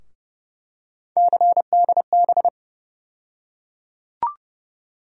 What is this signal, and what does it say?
Morse code, heard as a keyed tone, about 30 wpm, 700 Hz. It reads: CDB
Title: USA Call Sign Postfixes - 3 Letters 30wpm